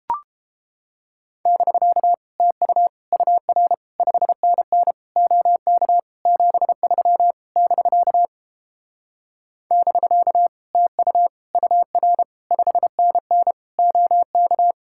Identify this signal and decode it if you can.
Morse code, heard as a keyed tone, about 31 wpm, 700 Hz. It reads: <BK> TU UR 5NN OK 73 <BK> <BK> TU UR 5NN OK